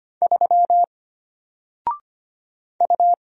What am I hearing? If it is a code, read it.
Morse code, heard as a keyed tone, about 25 wpm, 700 Hz. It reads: 3 U
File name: Single-Letter-Number-25wpm